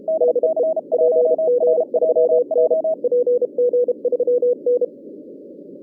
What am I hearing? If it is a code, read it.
Morse code, heard as a keyed tone, about 32 wpm, 490 Hz, with radio noise. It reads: DE PG3N PG3N
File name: 2011 bcc-challenge samstag tape